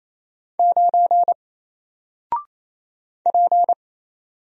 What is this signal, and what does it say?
Morse code, heard as a keyed tone, about 27 wpm, 700 Hz. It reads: 9 P
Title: Single Character (Letter:Number:Punctuation:Prosign) 28wpm